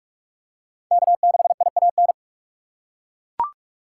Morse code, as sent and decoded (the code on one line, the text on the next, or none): -.- -.... .. .- -.
K6IAN